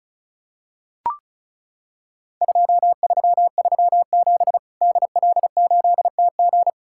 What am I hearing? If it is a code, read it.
Morse code, heard as a keyed tone, about 36 wpm, 700 Hz. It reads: 2337 DL8TG
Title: Worked All Europe DX Contest - QTC Exchange - 35wpm